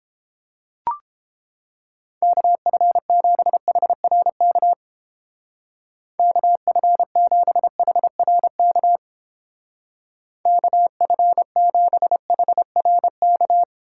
Morse code, heard as a keyed tone, about 31 wpm, 700 Hz. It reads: KF7HRK KF7HRK KF7HRK